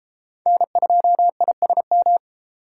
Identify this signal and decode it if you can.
Morse code, heard as a keyed tone, about 33 wpm, 700 Hz. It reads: N2ISM